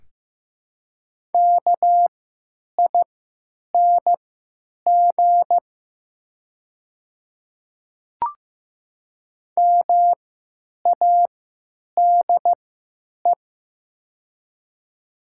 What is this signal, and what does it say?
Morse code, heard as a keyed tone, about 16 wpm, spaced as for 9 wpm, 700 Hz. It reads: KING MADE
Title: 3x Character Spacing - Top 500 Words - 15wpm